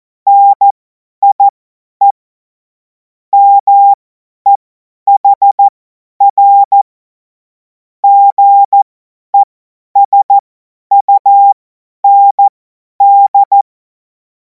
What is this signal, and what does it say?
Morse code, heard as a keyed tone, about 14 wpm, spaced as for 10 wpm, 800 Hz. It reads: NIE MEHR GESUND
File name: Morsekurs / Witze / Witz 04. August 2026 50 F70